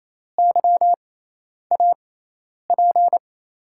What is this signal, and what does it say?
Morse code, heard as a keyed tone, about 28 wpm, spaced as for 10 wpm, 700 Hz. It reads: YAP
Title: USA Call Signs - 5 Characters - 6x Character Spacing - 28wpm